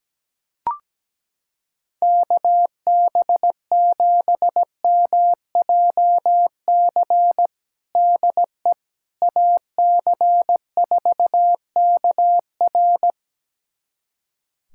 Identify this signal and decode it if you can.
Morse code, heard as a keyed tone, about 17 wpm, 700 Hz. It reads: KB7MJC DE AC4KR
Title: USA Callsign DE Callsign 17wpm